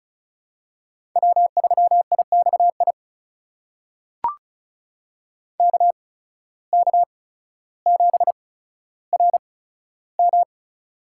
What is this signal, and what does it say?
Morse code, heard as a keyed tone, about 36 wpm, spaced as for 15 wpm, 700 Hz. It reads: W3IXI KK7RM
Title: USA Call Signs - 5 Characters - 8x Character Spacing - 35wpm